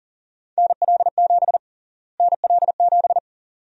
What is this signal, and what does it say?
Morse code, heard as a keyed tone, about 40 wpm, 700 Hz. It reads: NL7 NL7
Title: USA Call Sign Prefixes - Repeated 2x - 40wpm